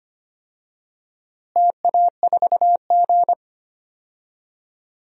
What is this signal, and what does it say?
Morse code, heard as a keyed tone, about 25 wpm, 700 Hz. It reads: TA4G
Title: DXCC-100 Call Signs - 4 Characters - Repeated 2x - 25wpm